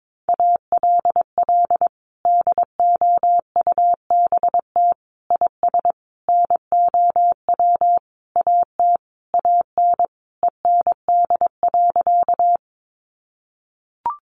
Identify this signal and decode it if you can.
Morse code, heard as a keyed tone, about 22 wpm, 700 Hz. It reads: ALL DOUBT IS NOW AT AN END.
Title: Sentences from Top 1500 Words 22wpm